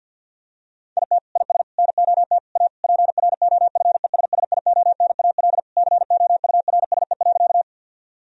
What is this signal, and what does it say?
Morse code, heard as a keyed tone, about 50 wpm, 700 Hz. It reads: IT IS NOT A PROFESSIONAL COURSE.
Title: Sentences from Top 1000 Words - Review Part 4 - 50wpm